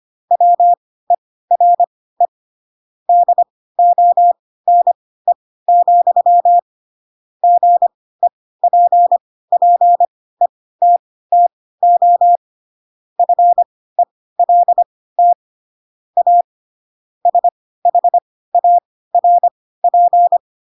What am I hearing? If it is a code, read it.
Morse code, heard as a keyed tone, about 25 wpm, spaced as for 16 wpm, 700 Hz. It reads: WERE DONE, GEPPETTO FELT A SHARP
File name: Pinocchio / Pinocchio ch3 10 25 ik1hge